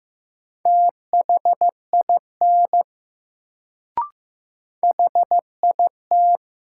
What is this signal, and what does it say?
Morse code, heard as a keyed tone, about 15 wpm, 700 Hz. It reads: THIN HIT
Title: Sentences from Top 1200 Words - New Word Drill - 15wpm